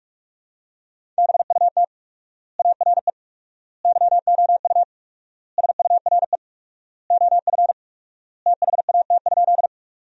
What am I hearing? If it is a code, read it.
Morse code, heard as a keyed tone, about 45 wpm, 700 Hz. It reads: BUT ARE YOU SURE OF THAT?
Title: 4x Word-Spacing - Sentences from Top 300 Words - 45wpm